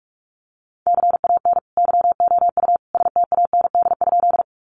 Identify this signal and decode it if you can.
Morse code, heard as a keyed tone, about 44 wpm, 700 Hz. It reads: CAN YOU STAND?
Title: Sentences from Top 300 Words 45wpm